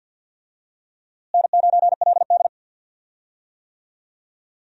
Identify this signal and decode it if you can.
Morse code, heard as a keyed tone, about 46 wpm, 700 Hz. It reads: N9LD
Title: USA Call Signs - 4 Characters - Repeated 3x - 50wpm